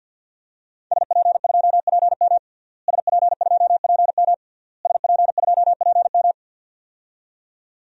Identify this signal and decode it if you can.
Morse code, heard as a keyed tone, about 51 wpm, 700 Hz. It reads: SP2PM SP2PM SP2PM
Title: DXCC-100 Call Signs - 5 Characters - Repeated 3x - 50wpm